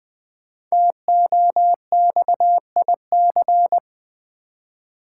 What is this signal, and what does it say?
Morse code, heard as a keyed tone, about 20 wpm, 700 Hz. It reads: TOXIC
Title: 5 Letter Words - First 3 Letters Given - 20wpm